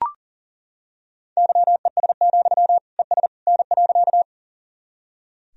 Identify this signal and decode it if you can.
Morse code, heard as a keyed tone, about 38 wpm, 700 Hz. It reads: YES, ES N.
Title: Sentences from Top 500 Words and Abbreviations 40wpm